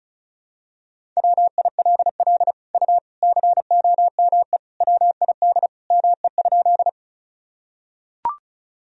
Morse code, heard as a keyed tone, about 34 wpm, 690 Hz. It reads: WILL U COME WID ME?
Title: Sentences from Top 200 Words and Abbreviations 35wpm